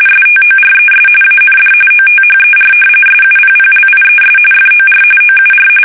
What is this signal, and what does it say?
RTTY, about 45.45 baud. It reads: STORES AND OTHER SUPPLIES FOR OUR
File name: fsk850 / usnavy1